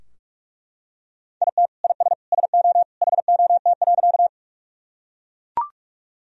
Morse code, heard as a keyed tone, about 41 wpm, 700 Hz. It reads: IT IS SO HOT.
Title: Sentences from Top 1000 Words - Review Part 3 - 45wpm